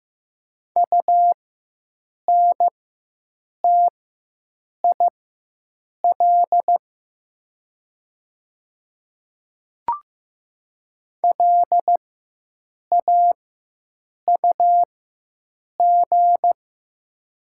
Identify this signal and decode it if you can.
Morse code, heard as a keyed tone, about 15 wpm, spaced as for 7 wpm, 690 Hz. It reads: UNTIL LAUG
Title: 4x Character Spacing - Top 500 Words - 15wpm